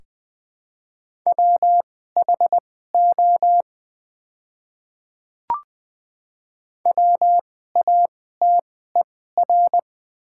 Morse code, heard as a keyed tone, about 21 wpm, spaced as for 15 wpm, 700 Hz. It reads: WHO WATER
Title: 2x Character Spacing - Top 100 Words - 20wpm